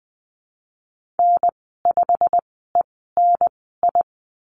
Morse code, heard as a keyed tone, about 20 wpm, spaced as for 14 wpm, 700 Hz. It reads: N5ENI
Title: 147 Lesson 24g - Letter y - Call Signs 20wpm